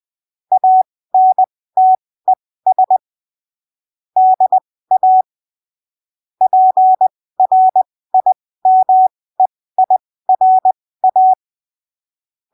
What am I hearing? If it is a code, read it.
Morse code, heard as a keyed tone, about 20 wpm, spaced as for 15 wpm, 750 Hz. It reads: ANTES DA PRIMEIRA
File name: arquivos / texto / texto 14 20ppm